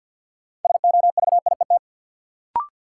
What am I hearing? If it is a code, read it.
Morse code, heard as a keyed tone, about 53 wpm, 700 Hz. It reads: SOVIET
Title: Sets of 5 Words Encore - New Word Drill - 50wpm